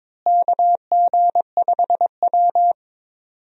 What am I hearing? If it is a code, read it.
Morse code, heard as a keyed tone, about 21 wpm, 700 Hz. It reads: KG5W
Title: USA Call Signs - 4 Characters - Repeated 2x - 22wpm